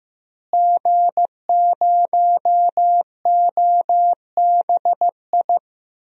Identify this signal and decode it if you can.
Morse code, heard as a keyed tone, about 15 wpm, 700 Hz. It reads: G0OBI
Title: DXCC-100 Call Signs - 5 Characters - Repeated 2x - 15wpm